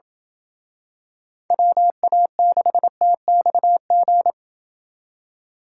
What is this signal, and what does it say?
Morse code, heard as a keyed tone, about 26 wpm, 700 Hz. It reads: WA6TXG